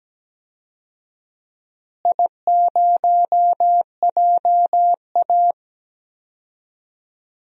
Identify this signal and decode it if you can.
Morse code, heard as a keyed tone, about 17 wpm, 700 Hz. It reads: I0JA
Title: DXCC-100 Call Signs - 4 Characters - Repeated 3x - 17wpm